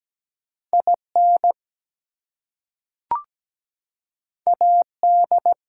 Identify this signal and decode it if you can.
Morse code, heard as a keyed tone, about 17 wpm, 700 Hz. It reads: IN AD
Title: Two Letter Words - 17wpm